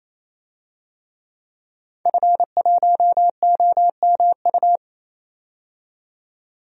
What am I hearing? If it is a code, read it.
Morse code, heard as a keyed tone, about 28 wpm, 700 Hz. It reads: F1OMU
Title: DXCC-100 Call Signs - 5 Characters - 28wpm